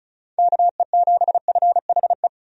Morse code, heard as a keyed tone, about 36 wpm, 700 Hz. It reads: KE7FHE
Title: USA Call Signs - 6 Characters - 35wpm